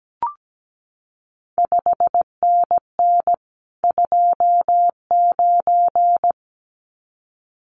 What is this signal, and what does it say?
Morse code, heard as a keyed tone, about 17 wpm, 700 Hz. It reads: 5NN 29
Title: CQ WW DX Contest Exchange Encore - 17wpm